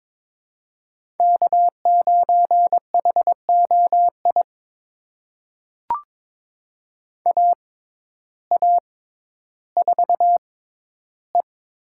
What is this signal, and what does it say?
Morse code, heard as a keyed tone, about 22 wpm, spaced as for 12 wpm, 700 Hz. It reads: K9HOI AA4E
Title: USA Call Signs - 5 Characters - 6x Character Spacing - 22wpm